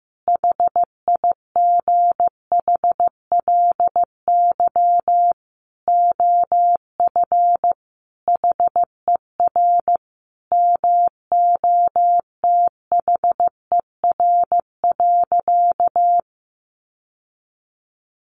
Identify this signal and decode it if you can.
Morse code, heard as a keyed tone, about 15 wpm, 700 Hz. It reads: HIGHLY OF HER MOTHER.
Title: Sentences from Top 1100 Words 15wpm